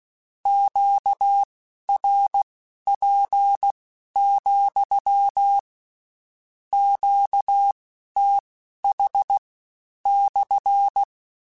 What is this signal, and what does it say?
Morse code, heard as a keyed tone, about 16 wpm, spaced as for 11 wpm, 800 Hz. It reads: QRP, QTH /
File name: Q CALLS SPSL @15